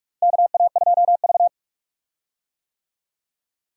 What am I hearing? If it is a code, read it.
Morse code, heard as a keyed tone, about 46 wpm, 700 Hz. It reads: KA2V
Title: USA Call Signs - 4 Characters - Repeated 3x - 45wpm